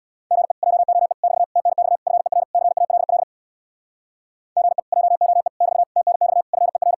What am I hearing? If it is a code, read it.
Morse code, heard as a keyed tone, about 67 wpm, 700 Hz. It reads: GE JOE <BT> TNX FER CALL GE JOE <BT> TNX FER